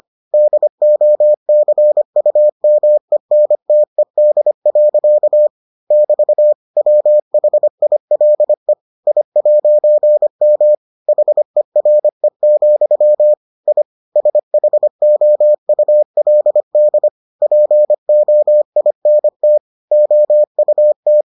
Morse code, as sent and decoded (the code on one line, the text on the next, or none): -.. --- -.-. ..- -- . -. - . -.. .-.-.- / -...- / .-- .... .. .-.. / .. .----. -- / .... .-. --..-- / .. / ... .... --- ..- .-.. -.. / .--. --- .. -. - / --- ..- -
DOCUMENTED. <BT> WHILE I'M HERE, I SHOULD POINT OUT